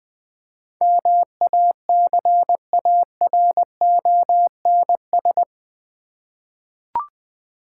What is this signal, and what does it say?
Morse code, heard as a keyed tone, about 20 wpm, 690 Hz. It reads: MACARONS
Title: Difficult Words with Hints - 20wpm